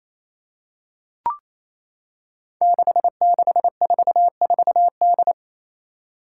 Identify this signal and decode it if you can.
Morse code, heard as a keyed tone, about 28 wpm, 700 Hz. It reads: 6644D
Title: Five Challenging Letter-Number Combinations 28wpm